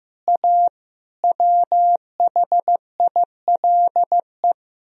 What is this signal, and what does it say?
Morse code, heard as a keyed tone, about 15 wpm, 700 Hz. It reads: A WHILE
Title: Speed-Racing - Sets of 5 Words 15wpm